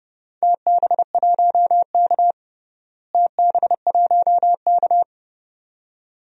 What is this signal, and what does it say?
Morse code, heard as a keyed tone, about 30 wpm, 700 Hz. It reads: TB1K TB1K
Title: DXCC-100 Call Signs - 4 Characters - Repeated 2x - 30wpm